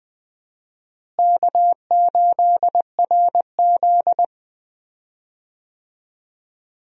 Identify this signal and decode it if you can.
Morse code, heard as a keyed tone, about 20 wpm, 700 Hz. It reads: K8RZ
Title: USA Call Signs - 4 Characters - Repeated 3x - 20wpm